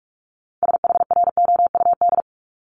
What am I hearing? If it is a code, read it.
Morse code, heard as a keyed tone, about 46 wpm, 700 Hz. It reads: SHROUD
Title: Multi-letter Phonemes Repeated 3x 45wpm